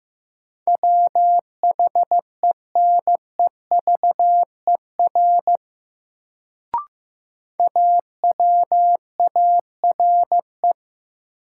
Morse code, heard as a keyed tone, about 15 wpm, 700 Hz. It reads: WHENEVER AWARE
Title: Sentences from Top 1200 Words - New Word Drill - 15wpm